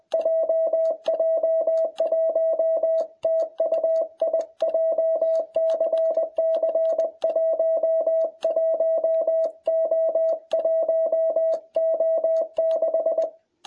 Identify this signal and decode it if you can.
Morse code, heard as a keyed tone, about 21 wpm, 660 Hz. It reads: JJ1TUI JCC 11O1O6